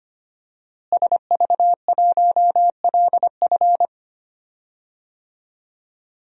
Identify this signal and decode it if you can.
Morse code, heard as a keyed tone, about 25 wpm, 700 Hz. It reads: SV1LF
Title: DXCC-100 Call Signs - 5 Characters - Repeated 3x - 25wpm